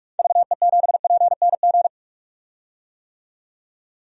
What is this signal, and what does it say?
Morse code, heard as a keyed tone, about 46 wpm, 700 Hz. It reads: VE7PNG
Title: DXCC-100 Call Signs - 6 Characters - 45wpm